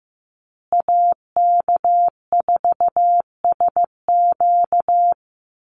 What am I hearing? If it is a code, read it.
Morse code, heard as a keyed tone, about 15 wpm, 700 Hz. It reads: AK4SQ
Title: Call Signs Repeated 2x - 15wpm